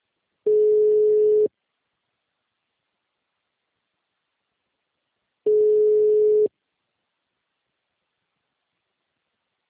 Ringback tone (Europe, 425 Hz).
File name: out- -2012- - - .2064